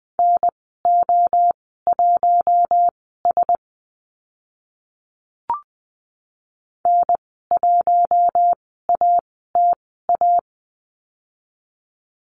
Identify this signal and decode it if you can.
Morse code, heard as a keyed tone, about 20 wpm, spaced as for 14 wpm, 700 Hz. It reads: NO1S N1ATA
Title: 042 Lesson 8g - Number 1 - Call Signs 20wpm